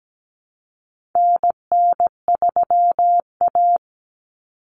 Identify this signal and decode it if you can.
Morse code, heard as a keyed tone, about 17 wpm, 700 Hz. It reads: NN3A
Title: Common Cut Numbers - 4-Digit 17wpm